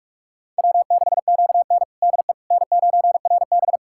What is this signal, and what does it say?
Morse code, heard as a keyed tone, about 41 wpm, 700 Hz. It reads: W6QN DE N9RB
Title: USA Callsign DE Callsign - 4 Characters - 45wpm